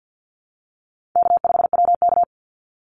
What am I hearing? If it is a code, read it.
Morse code, heard as a keyed tone, about 50 wpm, 700 Hz. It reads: K5WK
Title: Easy Call Signs Repeated 2x - 50wpm